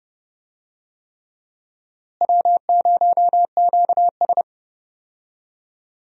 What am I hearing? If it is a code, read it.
Morse code, heard as a keyed tone, about 30 wpm, 700 Hz. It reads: W0QS
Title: USA Call Signs - 4 Characters - Repeated 3x - 30wpm